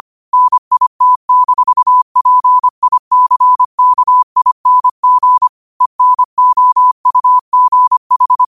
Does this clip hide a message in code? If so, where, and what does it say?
Yes: Morse code from 0.3 to 8.4 s, heard as a keyed tone, about 25 wpm, 1000 Hz. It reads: NIT-PICKING ENOUGH